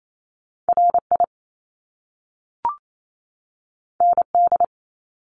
Morse code, heard as a keyed tone, about 28 wpm, 700 Hz. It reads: R I N D
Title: Top Two Letter Combinations in Words 28wpm